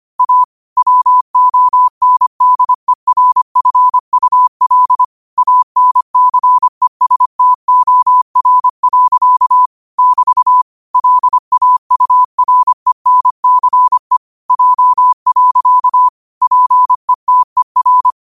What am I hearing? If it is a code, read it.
Morse code, heard as a keyed tone, about 25 wpm, 1000 Hz. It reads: A WONDERFUL ANCESTOR. <BT> LAURENCE J. PETER
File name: qotdfiles / 25wpm